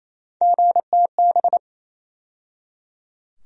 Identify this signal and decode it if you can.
Morse code, heard as a keyed tone, about 28 wpm, 700 Hz. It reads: GTB
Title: USA Call Sign Postfixes - 3 Letters 28wpm